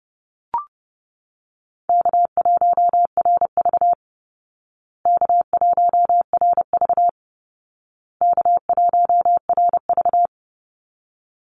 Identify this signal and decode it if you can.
Morse code, heard as a keyed tone, about 29 wpm, 700 Hz. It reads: K1RV K1RV K1RV
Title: Easy Call Signs Repeated 3x - 30wpm